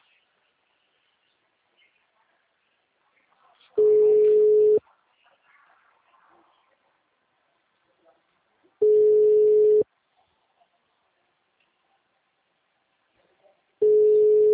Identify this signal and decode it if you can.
Ringback tone (Europe, 425 Hz).
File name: out- -2001- - - .32996